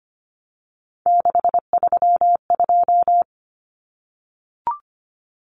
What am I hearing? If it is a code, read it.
Morse code, heard as a keyed tone, about 25 wpm, 700 Hz. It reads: 632
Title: Common Cut Numbers - 3-Digit 25wpm